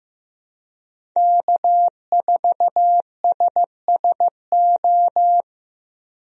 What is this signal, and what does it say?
Morse code, heard as a keyed tone, about 16 wpm, 700 Hz. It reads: K4SSO
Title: USA Call Signs - 5 Characters - Repeated 2x - 15wpm